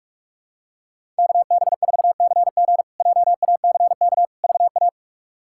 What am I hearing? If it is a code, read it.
Morse code, heard as a keyed tone, about 46 wpm, 700 Hz. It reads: KB4CG JACK VA
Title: K1USN SST QSO Practice - 45wpm